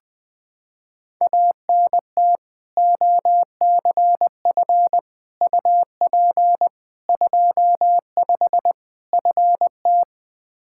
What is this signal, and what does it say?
Morse code, heard as a keyed tone, about 20 wpm, 700 Hz. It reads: ANT OCF UP 25 FT
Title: Practice QSO - Exchange 2 - Info - 20wpm